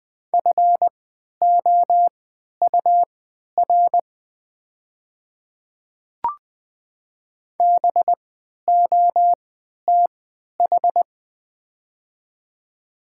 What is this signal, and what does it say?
Morse code, heard as a keyed tone, about 20 wpm, spaced as for 11 wpm, 700 Hz. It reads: FOUR BOTH
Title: 3x Character Spacing - Top 500 Words - 20wpm